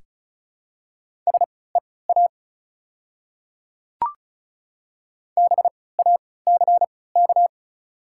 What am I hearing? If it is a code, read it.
Morse code, heard as a keyed tone, about 36 wpm, spaced as for 20 wpm, 690 Hz. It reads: SEA BACK
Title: 3x Character Spacing - Top 300 Words - 35wpm